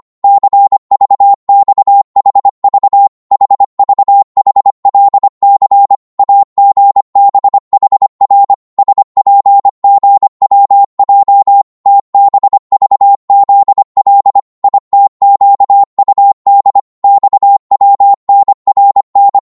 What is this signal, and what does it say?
Morse code, heard as a keyed tone, about 25 wpm, 800 Hz. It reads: CVXHV HVHLC AGBHR SPGWJ TBVZL ITQUD XWNRN
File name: Morse / ltr-25wpm